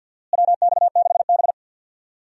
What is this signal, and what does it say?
Morse code, heard as a keyed tone, about 52 wpm, 700 Hz. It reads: WX6B
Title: USA Call Signs - 4 Characters - Repeated 2x - 50wpm